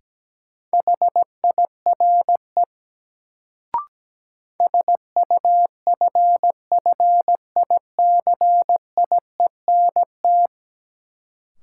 Morse code, heard as a keyed tone, about 17 wpm, 700 Hz. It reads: HIRE SUFFICIENT
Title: Sentences from Top 1400 Words - New Word Drill Alt - 17wpm